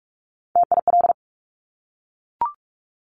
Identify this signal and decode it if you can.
Morse code, heard as a keyed tone, about 46 wpm, 690 Hz. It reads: TIL
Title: Top Three Letter Combinations in Words 45wpm